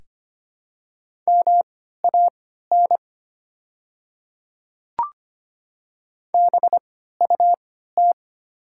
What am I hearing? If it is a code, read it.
Morse code, heard as a keyed tone, about 25 wpm, spaced as for 14 wpm, 700 Hz. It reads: MAN BUT
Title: 3x Character Spacing - Top 300 Words - 25wpm